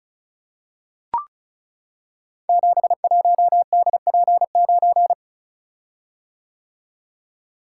Morse code, heard as a keyed tone, about 36 wpm, 700 Hz. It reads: 71DP9
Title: Code Groups - 5 Letter:Numbers - 35wpm